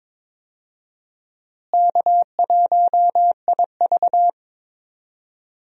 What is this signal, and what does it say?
Morse code, heard as a keyed tone, about 22 wpm, 700 Hz. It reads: K1IV
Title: USA Call Signs - 4 Characters - Repeated 3x - 22wpm